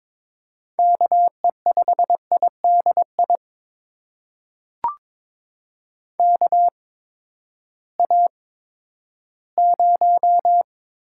Morse code, heard as a keyed tone, about 22 wpm, spaced as for 12 wpm, 700 Hz. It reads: KE5IDI KA0
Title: USA Call Signs - 6 Characters - 8x Character Spacing - 22wpm